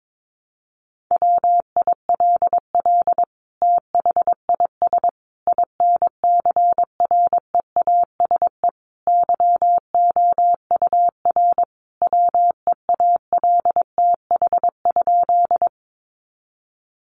Morse code, heard as a keyed tone, about 22 wpm, 700 Hz. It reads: WILL THIS INCREASE YOUR WEALTH?
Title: Sentences from Top 1200 Words 22wpm